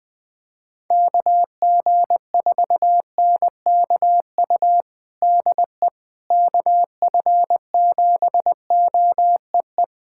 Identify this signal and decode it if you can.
Morse code, heard as a keyed tone, about 20 wpm, 700 Hz. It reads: KG4NKU DE KF7OEE
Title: USA Callsign DE Callsign - 6 Characters - 20wpm